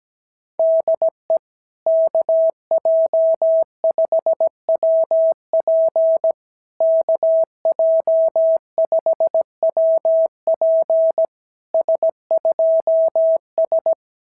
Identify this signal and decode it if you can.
Morse code, heard as a keyed tone, about 17 wpm, 650 Hz. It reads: DE KJ5WP KJ5WP S2S
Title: SOTA QSO Practice 17wpm